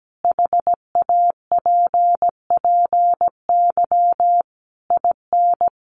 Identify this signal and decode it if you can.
Morse code, heard as a keyed tone, about 17 wpm, 700 Hz. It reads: HAPPY IN
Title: Sentences from Top 1000 Words 17wpm